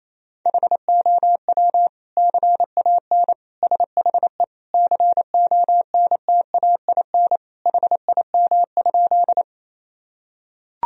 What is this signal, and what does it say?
Morse code, heard as a keyed tone, about 27 wpm, 700 Hz. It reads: HOW CAN SHE CONTAIN HIM?
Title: Sentences from Top 1000 Words - Review Part 2 - 28wpm